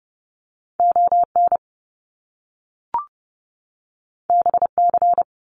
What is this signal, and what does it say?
Morse code, heard as a keyed tone, about 30 wpm, 700 Hz. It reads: ON BC
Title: Canadian Provinces and Territories 30wpm